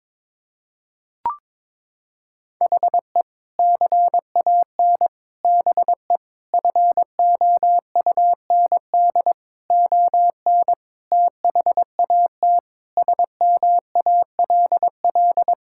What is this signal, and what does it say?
Morse code, heard as a keyed tone, about 23 wpm, 700 Hz. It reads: HE CAN BE FOUND ON THAT SMALL
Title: Sentences from Top 1600 Words 22wpm